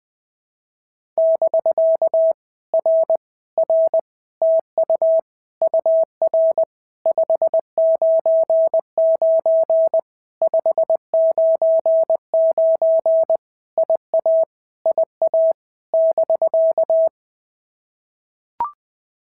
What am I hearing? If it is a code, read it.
Morse code, heard as a keyed tone, about 20 wpm, 650 Hz. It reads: <BK> R R TU UR 599 599 IA IA <BK>
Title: POTA QSO Practice - 20wpm